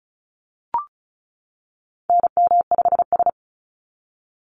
Morse code, heard as a keyed tone, about 34 wpm, 700 Hz. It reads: NM5S
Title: NA SOTA Chaser Callsigns 35wpm